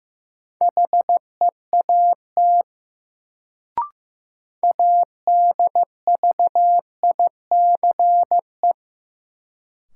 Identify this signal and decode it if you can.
Morse code, heard as a keyed tone, about 15 wpm, 700 Hz. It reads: HEAT ADVICE
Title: Sentences from Top 1100 Words - New Word Drill Alt - 15wpm